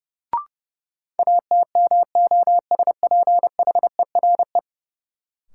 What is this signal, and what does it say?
Morse code, heard as a keyed tone, about 32 wpm, 700 Hz. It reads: ATMOSPHERE
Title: Sentences from Top 1600 Words - New Word Drill - 30wpm